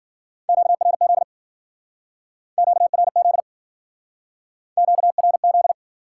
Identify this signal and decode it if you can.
Morse code, heard as a keyed tone, about 51 wpm, 700 Hz. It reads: QRZ QRZ QRZ